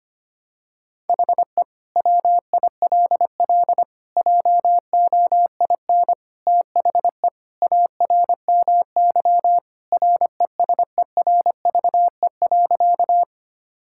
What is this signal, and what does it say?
Morse code, heard as a keyed tone, about 25 wpm, 700 Hz. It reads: HE WILL JOIN THE ARMY RESERVE.
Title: Sentences from Top 1000 Words - Review Part 6 - 25wpm